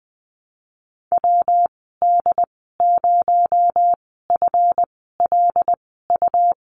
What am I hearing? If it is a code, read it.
Morse code, heard as a keyed tone, about 20 wpm, spaced as for 15 wpm, 700 Hz. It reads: WD0FLU
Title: 238 Lesson 38g - Letter q - Call Signs 20wpm